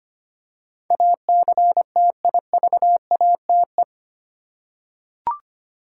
Morse code, heard as a keyed tone, about 25 wpm, 700 Hz. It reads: ACTIVATE
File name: Eight Letter Words - 25wpm